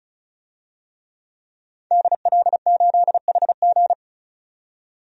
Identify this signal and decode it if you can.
Morse code, heard as a keyed tone, about 36 wpm, 700 Hz. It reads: DL8HG
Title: DXCC-100 Call Signs - 5 Characters - 35wpm